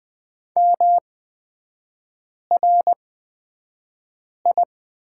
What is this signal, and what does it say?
Morse code, heard as a keyed tone, about 20 wpm, spaced as for 5 wpm, 690 Hz. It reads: MRI
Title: 236 Lesson 38e - Letter q - Three Characters 20wpm